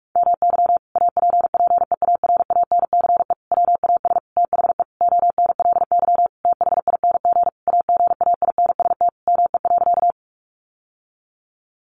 Morse code, heard as a keyed tone, about 44 wpm, 700 Hz. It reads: MY APPEARANCE WAS THE ONLY THING AGAINST ME.
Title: Sentences from Top 1100 Words 45wpm